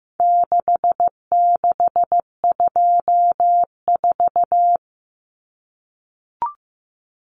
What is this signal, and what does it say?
Morse code, heard as a keyed tone, about 15 wpm, 700 Hz. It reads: 6624
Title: Numbers - 4-Digit 15wpm